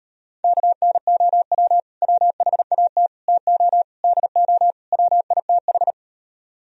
Morse code, heard as a keyed tone, about 33 wpm, 700 Hz. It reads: KNOW WHAT TO DO WITH